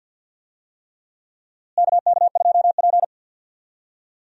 Morse code, heard as a keyed tone, about 53 wpm, 700 Hz. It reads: KK2P
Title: USA Call Signs - 4 Characters - Repeated 3x - 50wpm